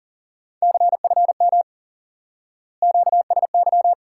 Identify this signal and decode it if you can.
Morse code, heard as a keyed tone, about 39 wpm, 700 Hz. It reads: CFM QSY
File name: IQE Course 3 - Lesson 89 - YAGI - Two Elements - 40wpm